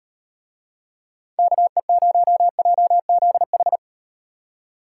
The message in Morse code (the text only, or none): KE0JZH